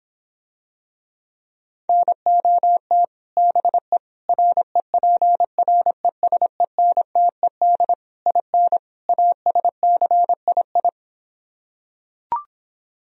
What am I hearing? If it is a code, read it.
Morse code, heard as a keyed tone, about 26 wpm, 700 Hz. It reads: NOT BE REPRESENTED IN ASCII